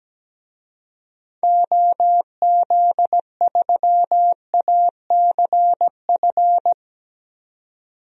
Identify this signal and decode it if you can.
Morse code, heard as a keyed tone, about 17 wpm, 700 Hz. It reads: OZ3ACF
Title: DXCC-100 Call Signs - 6 Characters - 17wpm